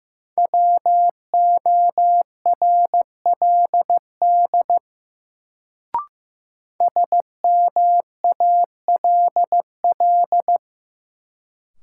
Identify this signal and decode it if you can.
Morse code, heard as a keyed tone, about 15 wpm, 700 Hz. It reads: WORLD SMALL
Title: Sentences from Top 200 Words - New Word Drill - 15wpm